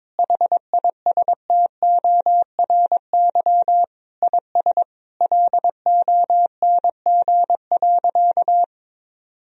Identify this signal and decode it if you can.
Morse code, heard as a keyed tone, about 22 wpm, 700 Hz. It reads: HISTORY IS LONG.
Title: Sentences from Top 1000 Words - Review Part 4 - 22wpm